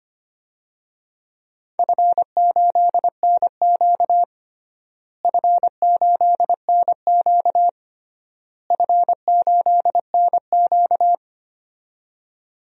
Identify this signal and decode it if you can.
Morse code, heard as a keyed tone, about 25 wpm, 700 Hz. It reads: F8NQ F8NQ F8NQ
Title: DXCC-100 Call Signs - 4 Characters - Repeated 3x - 25wpm